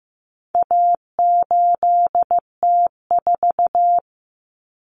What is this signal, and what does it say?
Morse code, heard as a keyed tone, about 15 wpm, 700 Hz. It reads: A8T4
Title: Common Cut Numbers - 4-Digit 15wpm